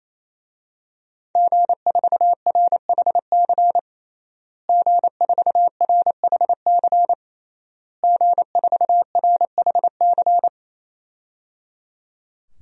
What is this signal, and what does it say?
Morse code, heard as a keyed tone, about 28 wpm, 700 Hz. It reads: G4RHC G4RHC G4RHC
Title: DXCC-100 Call Signs - 5 Characters - Repeated 3x - 28wpm